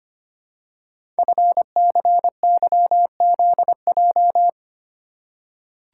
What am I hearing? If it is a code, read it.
Morse code, heard as a keyed tone, about 25 wpm, 700 Hz. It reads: FCYZJ
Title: Code Groups - 5 Letters - 25wpm